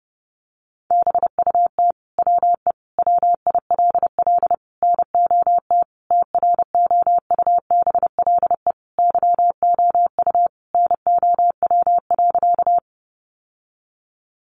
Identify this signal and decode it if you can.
Morse code, heard as a keyed tone, about 30 wpm, 700 Hz. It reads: BUT WE WILL NOT TROUBLE YOU NOW.
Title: Sentences from Top 1200 Words 30wpm